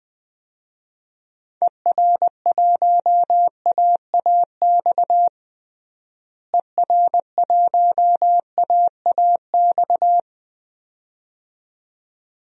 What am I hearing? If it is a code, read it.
Morse code, heard as a keyed tone, about 21 wpm, 700 Hz. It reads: ER1AAX ER1AAX
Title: DXCC-100 Call Signs - 6 Characters - Repeated 2x - 20wpm